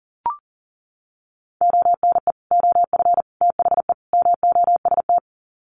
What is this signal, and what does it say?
Morse code, heard as a keyed tone, about 40 wpm, 700 Hz. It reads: ONE OF THE MOST
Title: Sets of 4 Words 40wpm